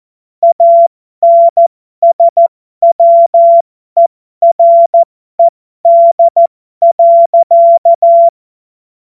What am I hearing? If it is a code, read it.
Morse code, heard as a keyed tone, about 14 wpm, spaced as for 12 wpm, 680 Hz. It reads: ANSWERED.